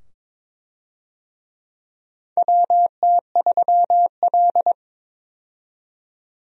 Morse code, heard as a keyed tone, about 22 wpm, 700 Hz. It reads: WT3L
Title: USA Call Signs - 4 Characters - 22wpm